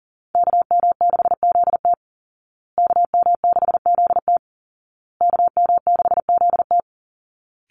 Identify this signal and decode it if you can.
Morse code, heard as a keyed tone, about 40 wpm, 700 Hz. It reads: KM6ZT KM6ZT KM6ZT
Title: Call Signs Repeated 2x - 40wpm